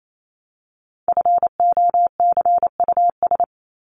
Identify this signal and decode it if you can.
Morse code, heard as a keyed tone, about 28 wpm, 700 Hz. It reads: FOCUS
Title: Top 501-750 Words 28wpm